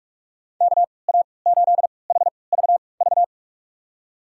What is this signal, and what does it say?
Morse code, heard as a keyed tone, about 46 wpm, spaced as for 26 wpm, 700 Hz. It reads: KA8HVV
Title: USA Call Signs - 6 Characters - 3x Character Spacing - 45wpm